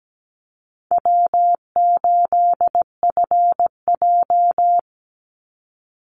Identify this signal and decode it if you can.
Morse code, heard as a keyed tone, about 17 wpm, 700 Hz. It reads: W8FJ
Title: Easy Call Signs - 17wpm